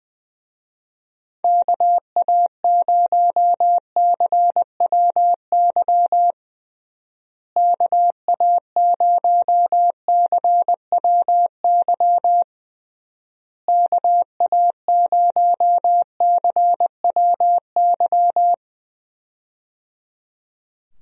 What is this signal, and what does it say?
Morse code, heard as a keyed tone, about 21 wpm, 700 Hz. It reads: KA0CWY KA0CWY KA0CWY
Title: USA Call Signs - 6 Characters - Repeated 3x - 20wpm